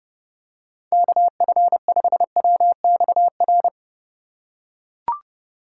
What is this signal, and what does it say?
Morse code, heard as a keyed tone, about 32 wpm, 700 Hz. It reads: KF5WXR
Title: USA Call Signs - 6 Characters - 2x Character Spacing - 30wpm